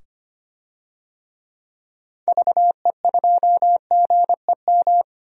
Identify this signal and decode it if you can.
Morse code, heard as a keyed tone, about 25 wpm, 700 Hz. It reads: VE2GEM
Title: DXCC-100 Call Signs - 6 Characters - 25wpm